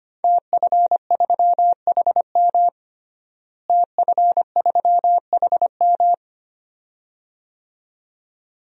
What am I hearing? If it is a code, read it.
Morse code, heard as a keyed tone, about 25 wpm, 700 Hz. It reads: TF3HM TF3HM
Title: DXCC-100 Call Signs - 5 Characters - Repeated 2x - 25wpm